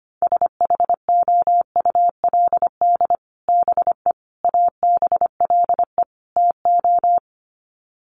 Morse code, heard as a keyed tone, about 25 wpm, 700 Hz. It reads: SHOULD BE ABLE TO
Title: Sets of 4 Words 25wpm